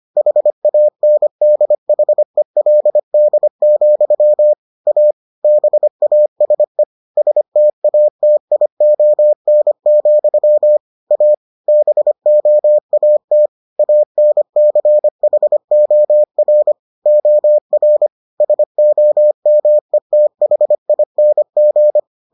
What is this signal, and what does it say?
Morse code, heard as a keyed tone, about 25 wpm, 600 Hz. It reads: HANDHELD, A BASE STATION, A BOAT ANCHOR OR SOMETHING